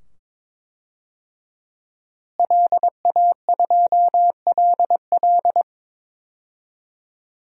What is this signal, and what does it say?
Morse code, heard as a keyed tone, about 22 wpm, 700 Hz. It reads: LA2LL
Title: DXCC-100 Call Signs - 5 Characters - Repeated 2x - 22wpm